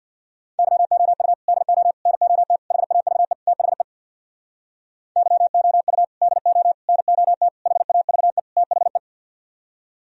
Morse code, heard as a keyed tone, about 64 wpm, 700 Hz. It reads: YOU DO NOT HAVE THE YOU DO NOT HAVE THE